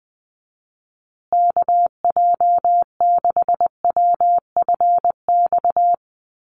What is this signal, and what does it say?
Morse code, heard as a keyed tone, about 20 wpm, 700 Hz. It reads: KJ6WFX
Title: USA Call Signs Repeated 3x 20wpm